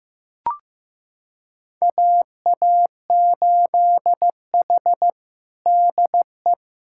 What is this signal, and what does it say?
Morse code, heard as a keyed tone, about 16 wpm, 700 Hz. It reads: AA8H DE
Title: USA Callsign DE Callsign - 4 Characters - 15wpm